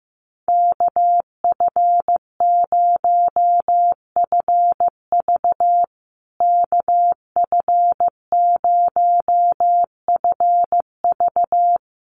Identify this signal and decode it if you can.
Morse code, heard as a keyed tone, about 14 wpm, 700 Hz. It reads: KF0FV KF0FV
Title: USA Call Signs Repeated 3x 15wpm